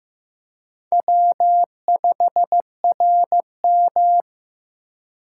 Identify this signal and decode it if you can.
Morse code, heard as a keyed tone, about 15 wpm, 700 Hz. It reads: W5RM
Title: USA Call Signs - 4 Characters - 2x Character Spacing - 15wpm